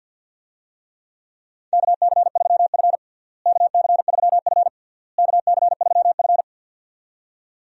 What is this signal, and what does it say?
Morse code, heard as a keyed tone, about 50 wpm, 700 Hz. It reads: KC3F KC3F KC3F
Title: USA Call Signs - 4 Characters - Repeated 3x - 50wpm